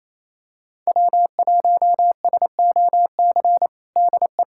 Morse code, heard as a keyed tone, about 28 wpm, 700 Hz. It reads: W1SOC DE
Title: USA Callsign DE Callsign - 5 Characters - 28wpm